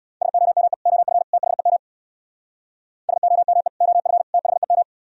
Morse code, heard as a keyed tone, about 62 wpm, 700 Hz. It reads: SOME OF THEM SOME OF THEM
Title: Speed-Racing - Sets of 3 Words 50wpm